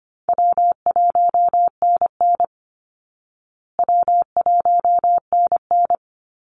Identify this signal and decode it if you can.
Morse code, heard as a keyed tone, about 25 wpm, 700 Hz. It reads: W1NN W1NN
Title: Easy Call Signs Repeated 3x - 25wpm